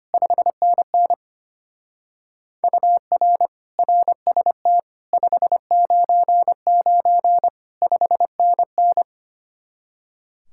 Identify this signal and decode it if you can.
Morse code, heard as a keyed tone, about 31 wpm, 700 Hz. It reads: 5NN UR RST 599 5NN
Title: Speed-Racing Common QSO Phrases - 25wpm